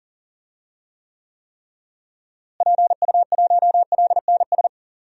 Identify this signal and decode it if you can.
Morse code, heard as a keyed tone, about 40 wpm, 700 Hz. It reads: PU1LNS
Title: DXCC-100 Call Signs - 6 Characters - Repeated 3x - 40wpm